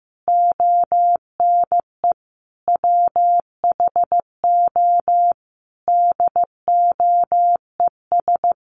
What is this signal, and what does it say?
Morse code, heard as a keyed tone, about 15 wpm, 700 Hz. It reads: ONE WHO DOES
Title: Sentences from Top 1000 Words 15wpm